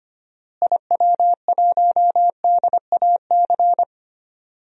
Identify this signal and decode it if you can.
Morse code, heard as a keyed tone, about 25 wpm, 700 Hz. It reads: IW1DAC
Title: DXCC-100 Call Signs - 6 Characters - Repeated 2x - 25wpm